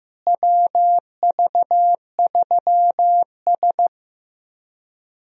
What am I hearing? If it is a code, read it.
Morse code, heard as a keyed tone, about 15 wpm, 700 Hz. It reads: WV3S
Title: USA Call Signs - 4 Characters - Repeated 3x - 15wpm